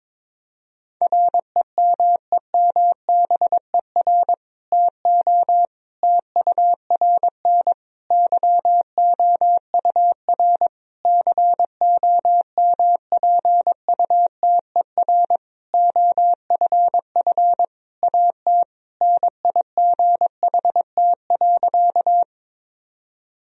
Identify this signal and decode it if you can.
Morse code, heard as a keyed tone, about 21 wpm, 700 Hz. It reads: REMEMBER TO TURN YOUR COMPUTER OFF AT NIGHT.
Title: Sentences from Top 1000 Words - Review Part 3 - 22wpm